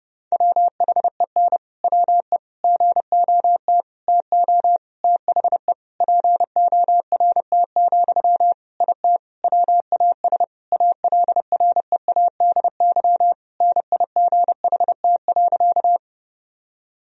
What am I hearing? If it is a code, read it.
Morse code, heard as a keyed tone, about 30 wpm, 700 Hz. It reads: WHEN WE GOT TO THE PORT, IT WAS ALREADY NIGHT.
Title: Sentences from Top 1000 Words - Review Part 1 - 30wpm